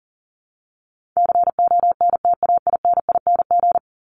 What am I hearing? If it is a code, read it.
Morse code, heard as a keyed tone, about 37 wpm, 700 Hz. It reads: CONTAINING
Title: ING Suffix - Top 100 Words with Base Words 40wpm